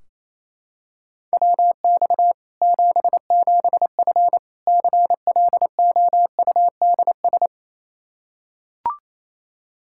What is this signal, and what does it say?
Morse code, heard as a keyed tone, about 27 wpm, 700 Hz. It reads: WX 77F CLOUDS
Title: Practice QSO - Exchange 2 - Info - 28wpm